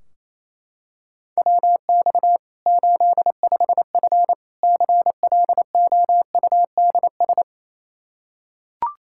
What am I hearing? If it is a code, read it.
Morse code, heard as a keyed tone, about 28 wpm, 700 Hz. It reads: WX 85F CLOUDS
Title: Practice QSO - Exchange 2 - Info - 28wpm